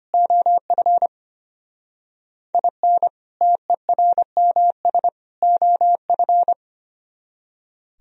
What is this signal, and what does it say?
Morse code, heard as a keyed tone, about 32 wpm, 700 Hz. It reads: OF IN TERMS OF